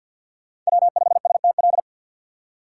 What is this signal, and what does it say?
Morse code, heard as a keyed tone, about 52 wpm, 700 Hz. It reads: W5STL
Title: USA Call Signs - 5 Characters - 2x Character Spacing - 50wpm